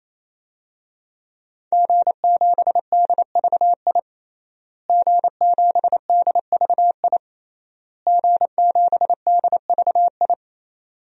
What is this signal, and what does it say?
Morse code, heard as a keyed tone, about 28 wpm, 700 Hz. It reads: G7DVI G7DVI G7DVI
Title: DXCC-100 Call Signs - 5 Characters - Repeated 3x - 28wpm